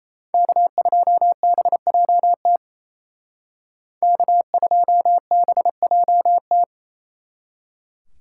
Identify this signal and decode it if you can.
Morse code, heard as a keyed tone, about 35 wpm, 700 Hz. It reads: K2BJT K2BJT